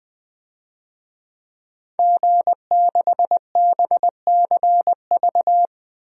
Morse code, heard as a keyed tone, about 20 wpm, 700 Hz. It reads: G6BCV
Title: DXCC-100 Call Signs - 5 Characters - Repeated 3x - 20wpm